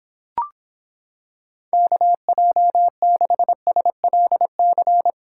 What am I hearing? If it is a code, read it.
Morse code, heard as a keyed tone, about 26 wpm, 700 Hz. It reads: KJ6SLC